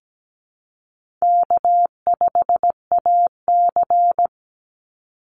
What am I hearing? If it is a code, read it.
Morse code, heard as a keyed tone, about 17 wpm, 700 Hz. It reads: K5AC
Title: USA Call Signs 17wpm